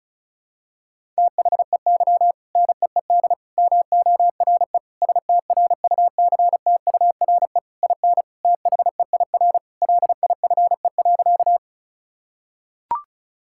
Morse code, heard as a keyed tone, about 36 wpm, 700 Hz. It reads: THEY NEED MORE STRUCTURE IN THEIR LIFE.
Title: Sentences from Top 1000 Words - Review Part 6 - 35wpm